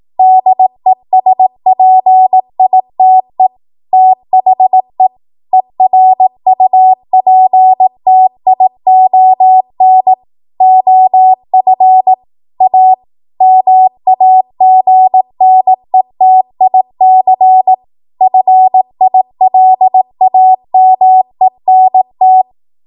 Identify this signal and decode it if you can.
Morse code, heard as a keyed tone, about 18 wpm, 750 Hz. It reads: DESPITE THE ERUPTION OF A MAGNETIC FILAMENT
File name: Archive / 18 WPM / 18WPM